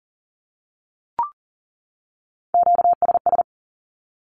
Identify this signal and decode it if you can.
Morse code, heard as a keyed tone, about 40 wpm, 700 Hz. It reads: QSS
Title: Q-Codes 40wpm